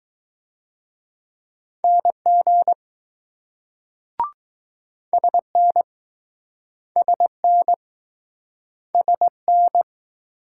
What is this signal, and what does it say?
Morse code, heard as a keyed tone, about 21 wpm, 700 Hz. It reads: NG SN SN SN